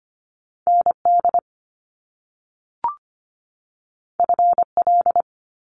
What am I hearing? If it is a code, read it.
Morse code, heard as a keyed tone, about 25 wpm, 700 Hz. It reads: ND FL
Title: US States 25wpm